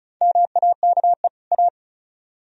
Morse code, heard as a keyed tone, about 48 wpm, spaced as for 42 wpm, 700 Hz. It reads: MAKE A
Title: Speed-Racing - Sets of 2 Words 35wpm